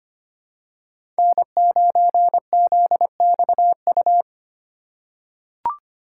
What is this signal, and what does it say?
Morse code, heard as a keyed tone, about 25 wpm, 700 Hz. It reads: N9ZXU
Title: USA Call Signs - 5 Characters - 4x Character Spacing - 25wpm